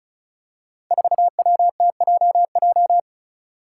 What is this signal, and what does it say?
Morse code, heard as a keyed tone, about 36 wpm, 700 Hz. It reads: 4WTJJ
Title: Five Challenging Letter-Number Combinations 35wpm